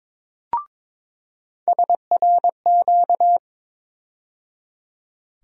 Morse code, heard as a keyed tone, about 22 wpm, 700 Hz. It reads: SRQ
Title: USA Call Sign Postfixes - 3 Letters 22wpm